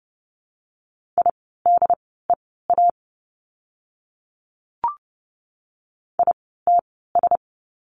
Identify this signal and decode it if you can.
Morse code, heard as a keyed tone, about 30 wpm, spaced as for 17 wpm, 700 Hz. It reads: IDEA ITS
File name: course / 30:10wpm / 108 Lesson 18f - Letter m - Words 30wpm